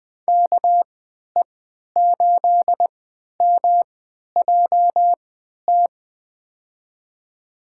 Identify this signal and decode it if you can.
Morse code, heard as a keyed tone, about 21 wpm, spaced as for 12 wpm, 700 Hz. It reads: KE8MJT
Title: USA Call Signs - 6 Characters - 3x Character Spacing - 20wpm